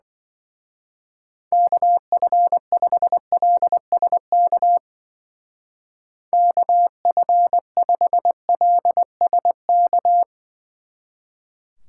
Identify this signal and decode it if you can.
Morse code, heard as a keyed tone, about 21 wpm, 700 Hz. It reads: KF5LSK KF5LSK
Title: Speed Racing - USA 2x3 Call Signs 20wpm